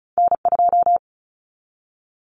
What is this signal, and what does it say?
Morse code, heard as a keyed tone, about 36 wpm, 700 Hz. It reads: N2
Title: Common Cut Numbers - 2-Digit 35wpm